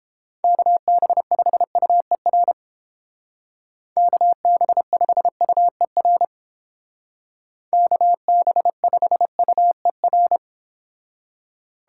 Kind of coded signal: Morse code

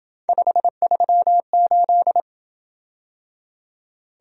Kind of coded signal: Morse code

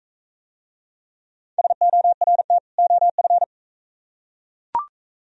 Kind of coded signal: Morse code

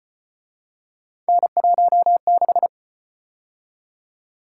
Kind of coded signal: Morse code

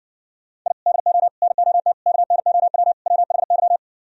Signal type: Morse code

